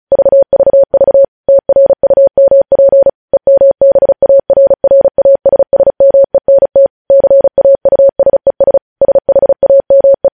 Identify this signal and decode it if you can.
Morse code, heard as a keyed tone, about 36 wpm, 550 Hz. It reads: VVV TRUMP EMBARRASSMENT CAUSES SHAME
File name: podcast / News headlines-1x-35x35